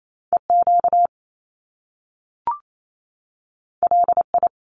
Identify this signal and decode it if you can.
Morse code, heard as a keyed tone, about 28 wpm, 700 Hz. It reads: EQ LI
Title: Top Two Letter Combinations in Words 28wpm